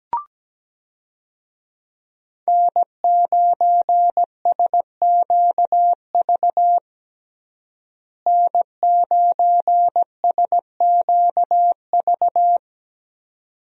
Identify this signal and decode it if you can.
Morse code, heard as a keyed tone, about 17 wpm, 700 Hz. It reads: N9SQV N9SQV
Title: USA Call Signs - 5 Characters - Repeated 2x - 17wpm